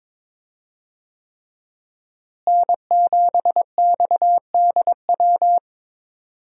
Morse code, heard as a keyed tone, about 22 wpm, 700 Hz. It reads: N7XDW
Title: USA Call Signs - 5 Characters - Repeated 3x - 22wpm